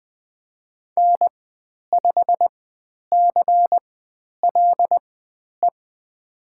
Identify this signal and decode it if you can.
Morse code, heard as a keyed tone, about 20 wpm, spaced as for 10 wpm, 700 Hz. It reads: N5CLE
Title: 102 Lesson 17g - Letter u - Call Signs 20wpm